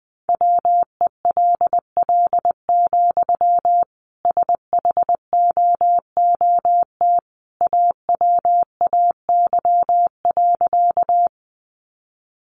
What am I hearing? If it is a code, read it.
Morse code, heard as a keyed tone, about 20 wpm, 700 Hz. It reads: WELL, SHOOT AWAY.
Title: Sentences from Top 1200 Words 20wpm